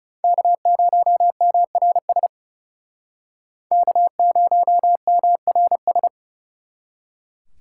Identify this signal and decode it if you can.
Morse code, heard as a keyed tone, about 40 wpm, 700 Hz. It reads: K0MRS K0MRS